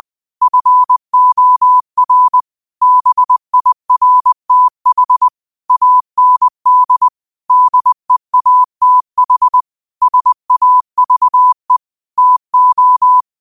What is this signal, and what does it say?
Morse code, heard as a keyed tone, about 20 wpm, 1000 Hz. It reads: FOR BIRTH AND DEATH SAVE TO
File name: qotdfiles / 20wpm